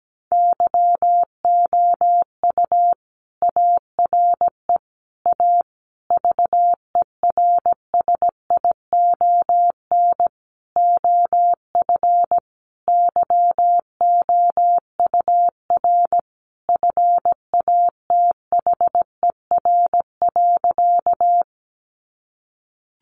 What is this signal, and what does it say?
Morse code, heard as a keyed tone, about 17 wpm, 700 Hz. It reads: YOU ARE A VERSION OF YOUR FATHER.
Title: Sentences from Top 1400 Words 17wpm